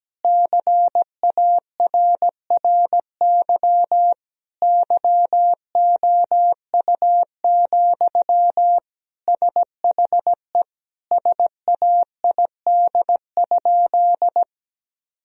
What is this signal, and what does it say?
Morse code, heard as a keyed tone, about 17 wpm, 700 Hz. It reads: CARRY YOU, SHE SAID?
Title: Sentences from Top 1000 Words - Review Part 1 - 17wpm